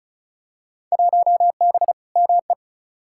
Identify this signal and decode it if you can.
Morse code, heard as a keyed tone, about 36 wpm, 700 Hz. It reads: 1B ME
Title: Field Day Exchanges - 35wpm